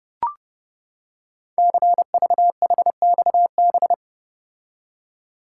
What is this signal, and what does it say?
Morse code, heard as a keyed tone, about 30 wpm, 700 Hz. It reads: CVHXB
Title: Code Groups - 5 Letters - 30wpm